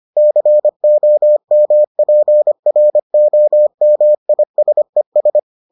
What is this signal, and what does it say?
Morse code, heard as a keyed tone, about 25 wpm, 600 Hz. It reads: COMPROMISES